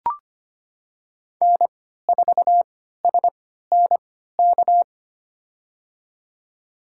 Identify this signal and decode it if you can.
Morse code, heard as a keyed tone, about 25 wpm, spaced as for 14 wpm, 700 Hz. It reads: N4SNK
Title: USA Call Signs - 5 Characters - 3x Character Spacing - 25wpm